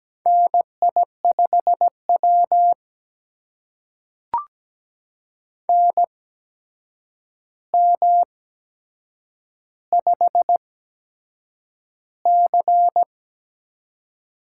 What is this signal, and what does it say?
Morse code, heard as a keyed tone, about 17 wpm, spaced as for 7 wpm, 700 Hz. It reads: NI5W NM5C
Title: USA Call Signs - 4 Characters - 8x Character Spacing - 17wpm